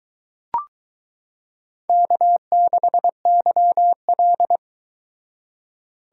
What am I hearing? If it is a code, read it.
Morse code, heard as a keyed tone, about 22 wpm, 700 Hz. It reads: K6YL